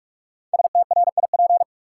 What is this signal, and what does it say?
Morse code, heard as a keyed tone, about 43 wpm, 700 Hz. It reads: STRIP
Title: Sentences from Top 1600 Words - New Word Drill Alt - 45wpm